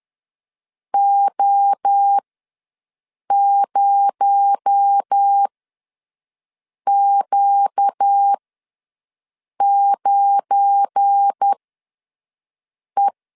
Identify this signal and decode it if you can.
Morse code, heard as a keyed tone, about 10 wpm, spaced as for 5 wpm, 800 Hz. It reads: O0Q9E
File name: LEZIONI / 7 GF9Z / 7 GF9 ws 10wpm 4034